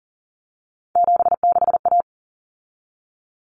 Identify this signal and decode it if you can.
Morse code, heard as a keyed tone, about 40 wpm, 700 Hz. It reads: 76A
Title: Common Cut Numbers - 3-Digit 40wpm